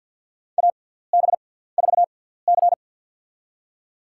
Morse code, heard as a keyed tone, about 48 wpm, spaced as for 17 wpm, 700 Hz. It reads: AB4C
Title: USA Call Signs - 4 Characters - 6x Character Spacing - 50wpm